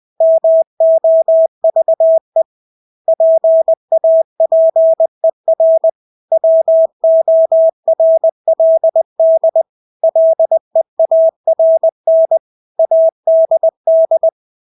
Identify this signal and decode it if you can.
Morse code, heard as a keyed tone, about 20 wpm, 650 Hz. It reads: MOVE PAPER WORLD LEARN ADD
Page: Engelske ord 20 wpm